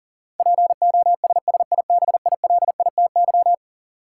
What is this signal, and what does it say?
Morse code, heard as a keyed tone, about 40 wpm, 700 Hz. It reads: POSSIBILITY
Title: Sets of 3 Words - New Word Drill - 40wpm